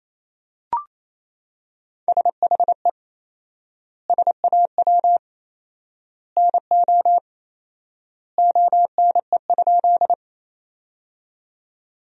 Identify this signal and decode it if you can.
Morse code, heard as a keyed tone, about 28 wpm, 700 Hz. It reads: SHE SAW NO ONE?
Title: 4x Word-Spacing - Sentences from Top 300 Words - 28wpm